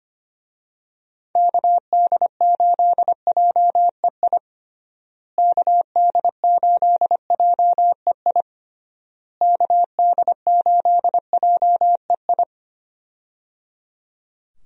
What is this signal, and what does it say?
Morse code, heard as a keyed tone, about 25 wpm, 700 Hz. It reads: KD8JEI KD8JEI KD8JEI
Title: USA Call Signs - 6 Characters - Repeated 3x - 25wpm